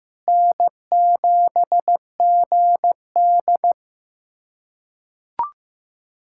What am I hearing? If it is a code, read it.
Morse code, heard as a keyed tone, about 15 wpm, 700 Hz. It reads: N7GD
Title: USA Callsign DE Callsign - 4 Characters - 15wpm